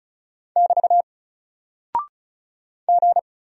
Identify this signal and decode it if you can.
Morse code, heard as a keyed tone, about 36 wpm, 700 Hz. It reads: <BT> G
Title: Single Character (Letter:Number:Punctuation:Prosign) Alt 35wpm